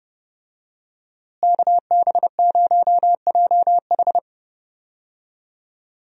Morse code, heard as a keyed tone, about 29 wpm, 700 Hz. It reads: KB0JH
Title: USA Call Signs - 5 Characters - Repeated 3x - 30wpm